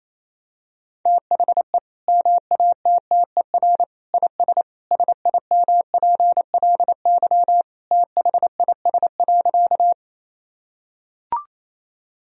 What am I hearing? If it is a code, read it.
Morse code, heard as a keyed tone, about 28 wpm, 700 Hz. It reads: THE MATTER IS SIMPLY THIS.
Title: Sentences from Top 1000 Words - Review Part 4 - 28wpm